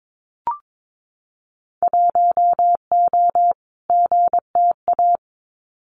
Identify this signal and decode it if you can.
Morse code, heard as a keyed tone, about 22 wpm, 700 Hz. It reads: 1O GTA
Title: Winter Field Day Exchanges 22wpm